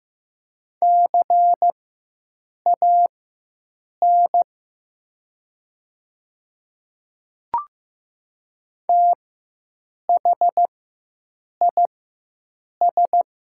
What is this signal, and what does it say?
Morse code, heard as a keyed tone, about 15 wpm, spaced as for 7 wpm, 700 Hz. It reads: CAN THIS
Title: 4x Character Spacing - Top 500 Words - 15wpm